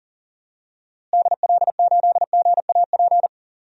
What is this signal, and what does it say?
Morse code, heard as a keyed tone, about 40 wpm, 700 Hz. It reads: DL8GAP
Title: DXCC-100 Call Signs - 6 Characters - Repeated 2x - 40wpm